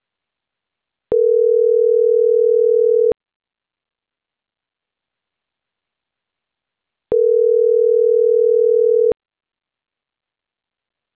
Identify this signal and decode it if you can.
Ringback tone (North America), 440 + 480 Hz.